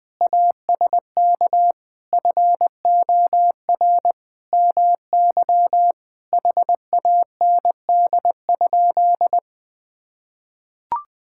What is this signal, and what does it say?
Morse code, heard as a keyed tone, about 20 wpm, 700 Hz. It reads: ASK FOR MY HAND?
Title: Sentences from Top 1600 Words 20wpm